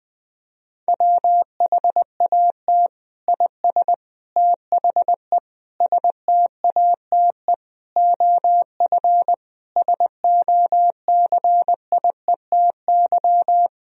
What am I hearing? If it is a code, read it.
Morse code, heard as a keyed tone, about 20 wpm, 700 Hz. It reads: WHAT IS THE STATE OF SOCIETY
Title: Sentences from Top 1000 Words - Review Part 4 - 20wpm